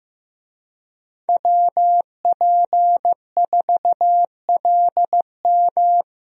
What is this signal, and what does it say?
Morse code, heard as a keyed tone, about 15 wpm, 700 Hz. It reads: WP4LM
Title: USA Call Signs - 5 Characters - 15wpm